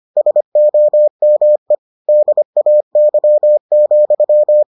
Morse code, heard as a keyed tone, about 25 wpm, 600 Hz. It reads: SOME DAY,